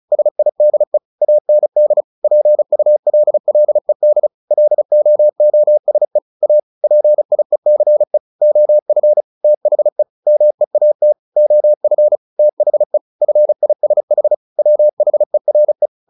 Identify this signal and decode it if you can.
Morse code, heard as a keyed tone, about 34 wpm, 600 Hz. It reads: SIDE AND PULLED LOOSE A PIECE OF THE MEAT OF THE FISH WHERE
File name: the old man and the sea 2 35wpm